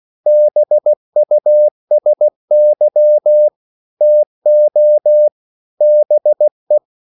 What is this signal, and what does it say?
Morse code, heard as a keyed tone, about 16 wpm, 600 Hz. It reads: BUSY TO BE
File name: CW / Practice / EnglishQuotations0216